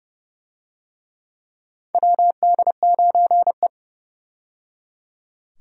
Morse code, heard as a keyed tone, about 30 wpm, 700 Hz. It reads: WD9E
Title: USA Call Signs - 4 Characters - 30wpm